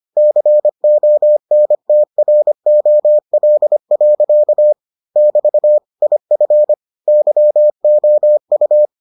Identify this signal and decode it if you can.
Morse code, heard as a keyed tone, about 25 wpm, 600 Hz. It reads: CONTROL. <BT> IF YOU